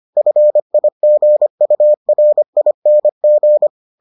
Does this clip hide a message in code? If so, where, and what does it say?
Yes: Morse code from 0.2 to 3.7 s, heard as a keyed tone, about 25 wpm, 600 Hz. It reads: FIGURING